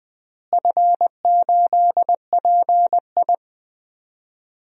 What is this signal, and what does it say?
Morse code, heard as a keyed tone, about 20 wpm, 700 Hz. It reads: F8PI
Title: DXCC-100 Call Signs - 4 Characters - Repeated 2x - 20wpm